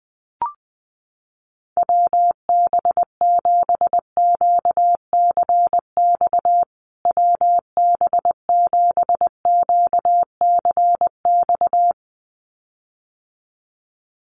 Morse code, heard as a keyed tone, about 20 wpm, 700 Hz. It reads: WB7QCX WB7QCX
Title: USA Call Signs Repeated 2x 20wpm